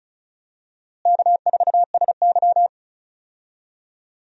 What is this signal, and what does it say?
Morse code, heard as a keyed tone, about 36 wpm, 700 Hz. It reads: K4SY
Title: USA Call Signs - 4 Characters - Repeated 2x - 35wpm